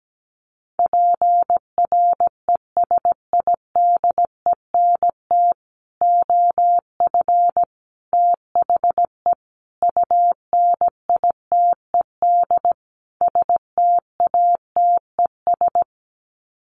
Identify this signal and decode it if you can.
Morse code, heard as a keyed tone, about 17 wpm, 700 Hz. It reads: PRESIDENT OF THE UNITED STATES
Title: Sets of 5 Words 17wpm